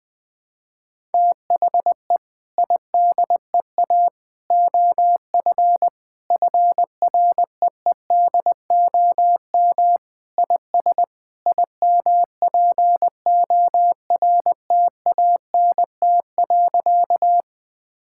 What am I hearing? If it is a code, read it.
Morse code, heard as a keyed tone, about 20 wpm, 700 Hz. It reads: THE IDEA OF FREEDOM IS IMPORTANT.
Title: Sentences with 3-Word Phrases - 20wpm